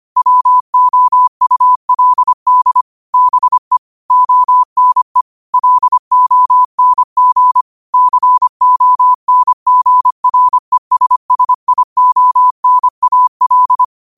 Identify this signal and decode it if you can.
Morse code, heard as a keyed tone, about 25 wpm, 1000 Hz. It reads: WOULD BE ONE LONG CONGRESSIONAL
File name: qotdfiles / 25wpm